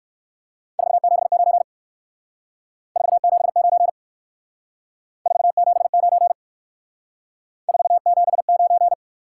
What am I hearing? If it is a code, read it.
Morse code, heard as a keyed tone, about 57 wpm, 700 Hz. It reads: 479 479 479 479